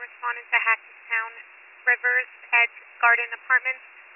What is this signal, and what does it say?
Emergency Alert System (SAME) tones, heard as harsh buzzing bursts, with speech. It is heard from 0.5 to 3.8 s.